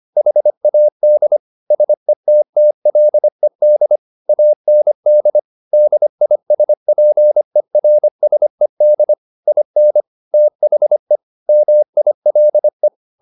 Morse code, heard as a keyed tone, about 25 wpm, 600 Hz. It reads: HAD SETTLED AND DISPERSED IN THE MILE